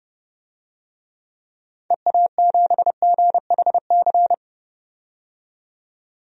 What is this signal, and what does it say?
Morse code, heard as a keyed tone, about 32 wpm, 700 Hz. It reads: EA7GHC
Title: DXCC-100 Call Signs - 6 Characters - 30wpm